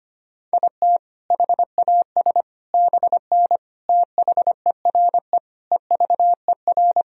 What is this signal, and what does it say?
Morse code, heard as a keyed tone, about 25 wpm, 700 Hz. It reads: IT HAS BN THERE EVER
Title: Sentences from Top 400 Words and Abbreviations 25wpm